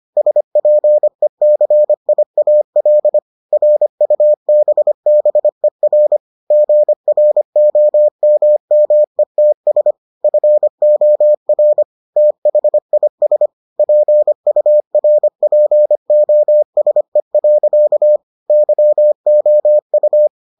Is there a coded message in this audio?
Yes: Morse code from 0.2 to 20.3 s, heard as a keyed tone, about 25 wpm, 600 Hz. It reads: SPECIAL RUBBER GROMMETS FOR THIS PURPOSE. YOU